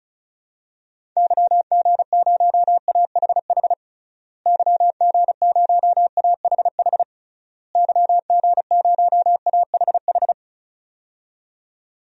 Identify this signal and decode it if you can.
Morse code, heard as a keyed tone, about 36 wpm, 700 Hz. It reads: YG0AHH YG0AHH YG0AHH
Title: DXCC-100 Call Signs - 6 Characters - Repeated 3x - 35wpm